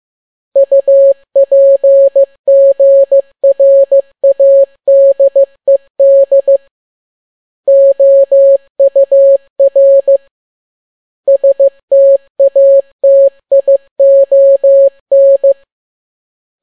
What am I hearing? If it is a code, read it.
Morse code, heard as a keyed tone, about 15 wpm, 550 Hz. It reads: UPGRADED OUR STATION
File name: n6tt p1 15C13